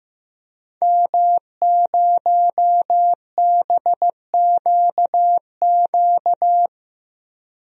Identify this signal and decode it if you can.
Morse code, heard as a keyed tone, about 15 wpm, 700 Hz. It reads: M0BQQ
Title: DXCC-100 Call Signs - 5 Characters - Repeated 2x - 15wpm